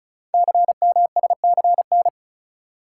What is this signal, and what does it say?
Morse code, heard as a keyed tone, about 36 wpm, 700 Hz. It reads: CMSCN
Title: Code Groups - 5 Letters - 35wpm